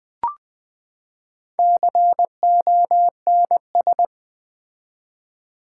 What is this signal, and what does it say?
Morse code, heard as a keyed tone, about 20 wpm, 700 Hz. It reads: CONS
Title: Build A Word - 4 Letters At A Time - 12-Letter Words - 20wpm